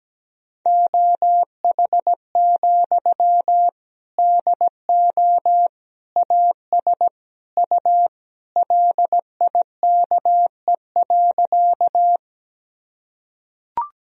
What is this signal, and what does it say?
Morse code, heard as a keyed tone, about 17 wpm, 700 Hz. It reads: OH, DO AS U LIKE.
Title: Sentences from Top 500 Words and Abbreviations 17wpm